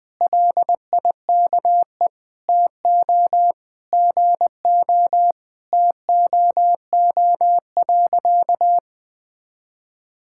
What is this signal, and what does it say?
Morse code, heard as a keyed tone, about 20 wpm, 700 Hz. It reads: LIKE TO GO TOO.
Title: Sentences with 3-Word Phrases - Spoken Beginning - 20wpm